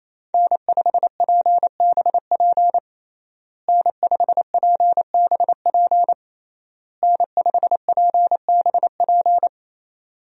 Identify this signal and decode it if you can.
Morse code, heard as a keyed tone, about 28 wpm, 700 Hz. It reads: N5PBP N5PBP N5PBP
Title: USA Call Signs - 5 Characters - Repeated 3x - 28wpm